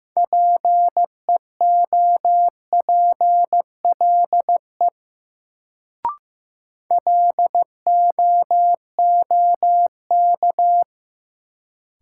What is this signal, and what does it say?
Morse code, heard as a keyed tone, about 15 wpm, 700 Hz. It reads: PEOPLE LOOK
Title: Sets of 2 Words - New Word Drill - 15wpm